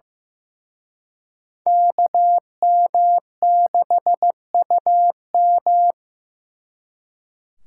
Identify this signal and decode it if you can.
Morse code, heard as a keyed tone, about 18 wpm, 700 Hz. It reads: KM6UM
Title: Speed Racing - USA 2x2 Call Signs 15wpm